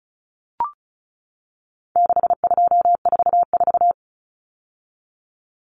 Morse code, heard as a keyed tone, about 34 wpm, 700 Hz. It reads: 6244
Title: Numbers - 4-Digit 35wpm